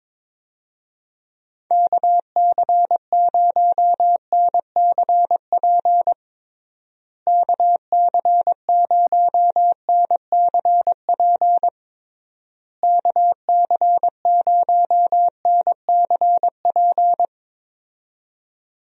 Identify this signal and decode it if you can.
Morse code, heard as a keyed tone, about 22 wpm, 700 Hz. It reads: KC0NCP KC0NCP KC0NCP
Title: USA Call Signs - 6 Characters - Repeated 3x - 22wpm